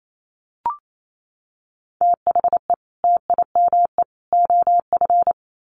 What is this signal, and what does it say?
Morse code, heard as a keyed tone, about 28 wpm, 700 Hz. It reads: THE TIME OF
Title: Sets of 3 Words 28wpm